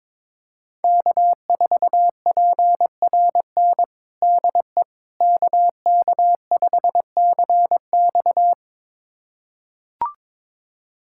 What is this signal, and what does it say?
Morse code, heard as a keyed tone, about 21 wpm, 700 Hz. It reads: K4PRN DE KK5CX
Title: USA Callsign DE Callsign - 5 Characters - 22wpm